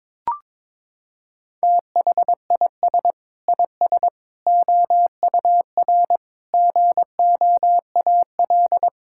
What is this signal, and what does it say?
Morse code, heard as a keyed tone, about 21 wpm, 700 Hz. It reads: THIS IS OUR GOAL
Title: Sentences from Top 1000 Words - Review Part 5 - 22wpm